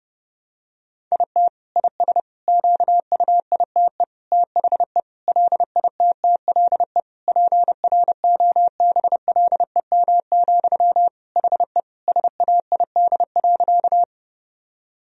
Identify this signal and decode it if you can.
Morse code, heard as a keyed tone, about 32 wpm, 700 Hz. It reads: IT IS QUITE THE LITTLE PROBLEM, HE SAID.
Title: Sentences from Top 1000 Words - Review Part 2 - 30wpm